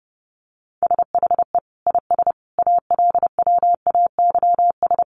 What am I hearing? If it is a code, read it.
Morse code, heard as a keyed tone, about 30 wpm, 700 Hz. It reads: SHE IS ALWAYS
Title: Sentences from Top 1400 Words 30wpm